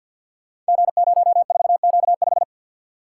Morse code, heard as a keyed tone, about 46 wpm, 700 Hz. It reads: G04Q5
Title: Code Groups - 5 Letter:Numbers - 50wpm